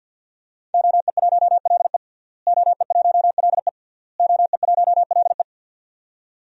Morse code, heard as a keyed tone, about 46 wpm, 700 Hz. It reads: OE1LE OE1LE OE1LE
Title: DXCC-100 Call Signs - 5 Characters - Repeated 3x - 50wpm